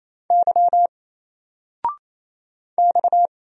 Morse code, heard as a keyed tone, about 27 wpm, 700 Hz. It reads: Y X
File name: Single-Letter-Number-28wpm